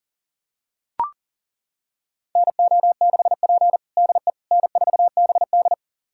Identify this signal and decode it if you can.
Morse code, heard as a keyed tone, about 40 wpm, 690 Hz. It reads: NO6P DE N4BD
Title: USA Callsign DE Callsign - 4 Characters - 40wpm